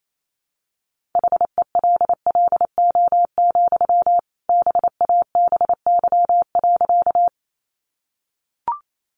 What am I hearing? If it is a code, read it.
Morse code, heard as a keyed tone, about 28 wpm, 700 Hz. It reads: HELLO, BABY.
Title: Sentences from Top 1300 Words 28wpm